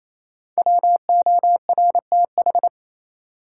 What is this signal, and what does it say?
Morse code, heard as a keyed tone, about 28 wpm, 700 Hz. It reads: WORTH
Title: Sentences from Top 800 Words - New Word Drill - 28wpm